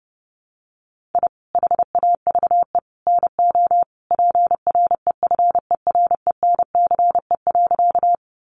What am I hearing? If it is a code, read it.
Morse code, heard as a keyed tone, about 30 wpm, 700 Hz. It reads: I HAVE NO PREFERENCE.
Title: Sentences from Top 1500 Words 30wpm